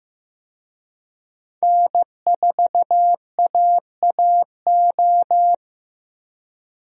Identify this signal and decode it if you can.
Morse code, heard as a keyed tone, about 15 wpm, 700 Hz. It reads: N4AAO
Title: USA Call Signs - 5 Characters - 15wpm